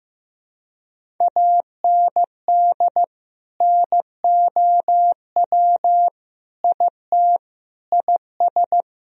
Morse code, heard as a keyed tone, about 16 wpm, 700 Hz. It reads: AND NOW IT IS
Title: Sentences from Top 1000 Words - Review Part 1 - 15wpm